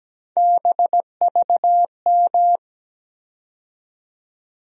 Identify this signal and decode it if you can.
Morse code, heard as a keyed tone, about 17 wpm, 700 Hz. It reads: BVM
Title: USA Call Sign Postfixes - 3 Letters 17wpm